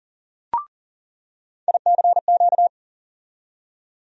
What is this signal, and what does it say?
Morse code, heard as a keyed tone, about 39 wpm, 700 Hz. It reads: ICQ
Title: USA Call Sign Postfixes - 3 Letters 40wpm